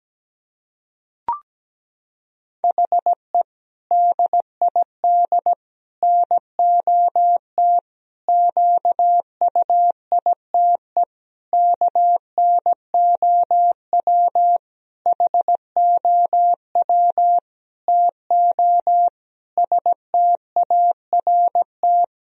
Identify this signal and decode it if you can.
Morse code, heard as a keyed tone, about 17 wpm, 700 Hz. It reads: HE DID NOT QUITE KNOW HOW TO START
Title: Sentences from Top 1000 Words - Review Part 1 - 17wpm